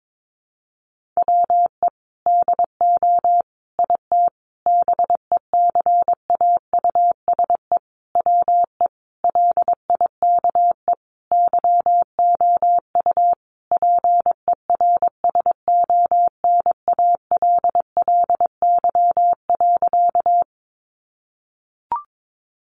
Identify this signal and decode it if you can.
Morse code, heard as a keyed tone, about 22 wpm, 700 Hz. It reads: WE DO IT BECAUSE WE LIKE YOU PERSONALLY.
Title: Sentences from Top 1400 Words 22wpm